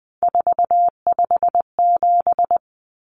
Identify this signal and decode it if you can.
Morse code, heard as a keyed tone, about 20 wpm, 700 Hz. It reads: 457
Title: Numbers - 3-Digit 20wpm